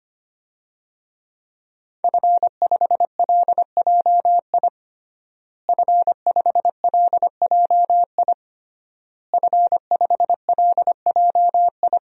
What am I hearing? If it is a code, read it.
Morse code, heard as a keyed tone, about 25 wpm, 700 Hz. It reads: F5LJI F5LJI F5LJI
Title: DXCC-100 Call Signs - 5 Characters - Repeated 3x - 25wpm